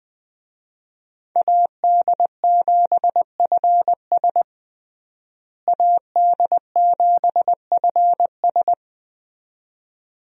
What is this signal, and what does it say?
Morse code, heard as a keyed tone, about 21 wpm, 700 Hz. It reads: AD7FS AD7FS
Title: USA Call Signs - 5 Characters - Repeated 2x - 20wpm